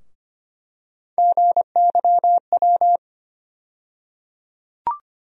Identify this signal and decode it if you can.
Morse code, heard as a keyed tone, about 25 wpm, 700 Hz. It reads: GYW
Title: USA Call Sign Postfixes - 3 Letters 25wpm